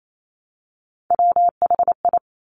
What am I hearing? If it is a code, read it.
Morse code, heard as a keyed tone, about 28 wpm, 700 Hz. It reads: WHI
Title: Top Three Letter Combinations in Words 28wpm